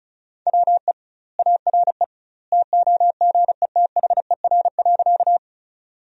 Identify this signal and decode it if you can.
Morse code, heard as a keyed tone, about 36 wpm, 700 Hz. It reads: WE ARE TOGETHER.
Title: 2x Word-Spacing - Sentences from Top 300 Words - 35wpm